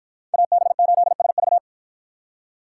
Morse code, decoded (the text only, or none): AB8SV